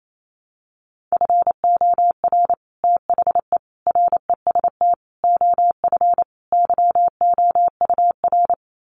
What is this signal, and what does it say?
Morse code, heard as a keyed tone, about 28 wpm, 700 Hz. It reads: FOR THE REST OF YOUR
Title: Sets of 5 Words Encore 28wpm